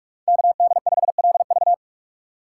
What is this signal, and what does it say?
Morse code, heard as a keyed tone, about 48 wpm, 700 Hz. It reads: KD5LV
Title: USA Call Signs - 5 Characters - 4x Character Spacing - 45wpm